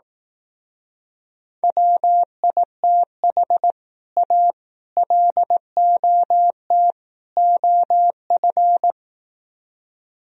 Morse code, heard as a keyed tone, about 18 wpm, 700 Hz. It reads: WITH A LOT OF
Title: Speed-Racing - Sets of 4 Words 15wpm